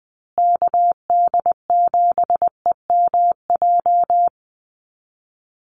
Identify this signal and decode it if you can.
Morse code, heard as a keyed tone, about 20 wpm, 700 Hz. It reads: KD7EMJ
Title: USA Call Signs Repeated 3x 20wpm